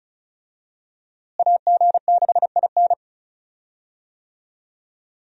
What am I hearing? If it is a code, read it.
Morse code, heard as a keyed tone, about 34 wpm, 700 Hz. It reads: AG6IN
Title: USA Call Signs - 5 Characters - Repeated 3x - 35wpm